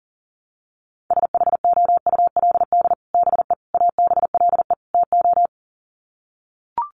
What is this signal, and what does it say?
Morse code, heard as a keyed tone, about 37 wpm, 700 Hz. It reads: SHOULD BE ABLE TO
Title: Sets of 4 Words 40wpm